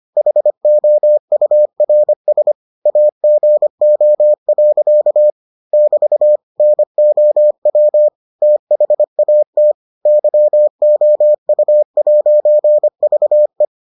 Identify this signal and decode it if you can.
Morse code, heard as a keyed tone, about 25 wpm, 600 Hz. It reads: HOURS AGO. <BT> NOW THAT YOU'VE